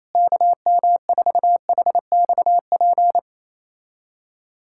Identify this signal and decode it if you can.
Morse code, heard as a keyed tone, about 27 wpm, 700 Hz. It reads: KM4HXP
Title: USA Callsign DE Callsign 28wpm